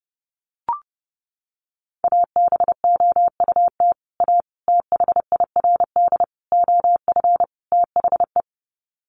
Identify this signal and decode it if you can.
Morse code, heard as a keyed tone, about 28 wpm, 700 Hz. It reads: ABOUT A THIRD OF THE
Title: Sets of 5 Words Encore2 30wpm